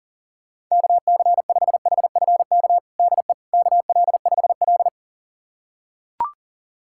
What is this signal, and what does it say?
Morse code, heard as a keyed tone, about 39 wpm, 700 Hz. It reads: KC5HFK DE KL5L
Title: USA Callsign DE Callsign 40wpm